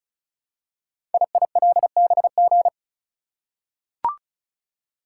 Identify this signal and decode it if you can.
Morse code, heard as a keyed tone, about 36 wpm, 700 Hz. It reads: IILBG
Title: Code Groups - 5 Letters - 35wpm